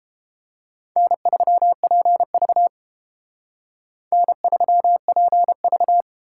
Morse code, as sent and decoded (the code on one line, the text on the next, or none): -. ...-- .--. ...- / -. ...-- .--. ...-
N3PV N3PV